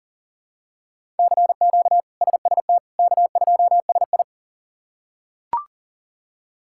Morse code, heard as a keyed tone, about 37 wpm, 700 Hz. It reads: CQ SST K2SI
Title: K1USN SST QSO Practice - 40wpm